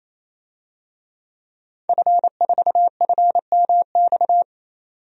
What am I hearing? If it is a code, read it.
Morse code, heard as a keyed tone, about 27 wpm, 700 Hz. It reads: F4FMX
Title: DXCC-100 Call Signs - 5 Characters - 28wpm